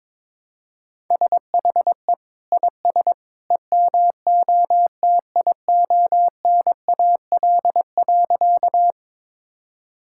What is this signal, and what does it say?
Morse code, heard as a keyed tone, about 22 wpm, 700 Hz. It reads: SHE IS EMOTIONAL.
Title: Sentences from Top 1000 Words - Review Part 5 - 22wpm